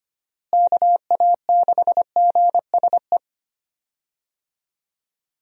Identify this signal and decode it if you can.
Morse code, heard as a keyed tone, about 25 wpm, 700 Hz. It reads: KA6GSE
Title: USA Call Signs - 6 Characters - Repeated 3x - 25wpm